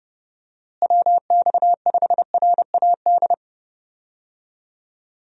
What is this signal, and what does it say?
Morse code, heard as a keyed tone, about 32 wpm, 700 Hz. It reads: WX5RAD
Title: USA Call Signs - 6 Characters - Repeated 3x - 30wpm